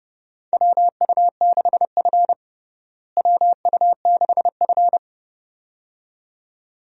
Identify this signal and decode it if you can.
Morse code, heard as a keyed tone, about 32 wpm, 700 Hz. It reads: WU6F WU6F
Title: USA Call Signs - 4 Characters - Repeated 2x - 30wpm